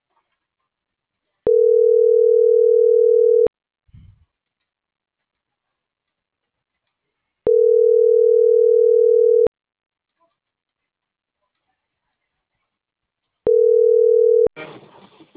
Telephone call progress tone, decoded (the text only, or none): ringback tone